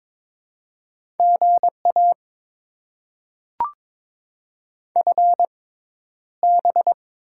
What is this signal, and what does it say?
Morse code, heard as a keyed tone, about 22 wpm, spaced as for 11 wpm, 700 Hz. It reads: GA FB
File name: Top 90 QSO Elements - 6x Character Spacing - 22wpm